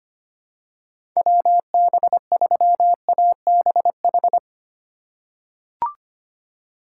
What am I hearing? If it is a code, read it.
Morse code, heard as a keyed tone, about 25 wpm, 700 Hz. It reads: WB3ABH
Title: USA Call Signs - 6 Characters - 2x Character Spacing - 25wpm